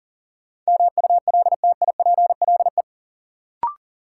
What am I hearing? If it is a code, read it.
Morse code, heard as a keyed tone, about 40 wpm, 700 Hz. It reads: MULTIPLE
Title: Sets of 5 Words - New Word Drill - 40wpm